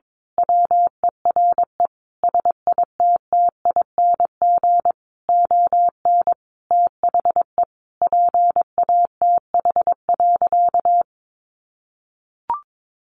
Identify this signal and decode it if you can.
Morse code, heard as a keyed tone, about 22 wpm, 700 Hz. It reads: WERE SITTING ON THE PATH.
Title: A Sound Of Thunder 22wpm